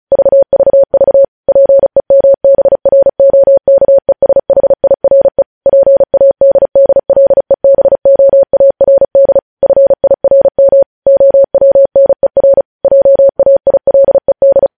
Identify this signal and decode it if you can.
Morse code, heard as a keyed tone, about 36 wpm, 550 Hz. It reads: VVV PEMBROKESHIRE PADDLEBOARD FIRM OWNER JAILED
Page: news headlines in morse code at 35 wpm